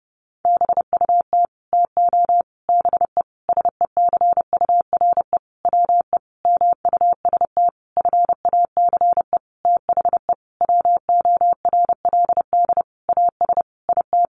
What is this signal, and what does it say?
Morse code, heard as a keyed tone, about 30 wpm, 700 Hz. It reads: BUT TO BE SECURE WE MUST FACE THE WORLD AS IT
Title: Sentences from Top 1200 Words 30wpm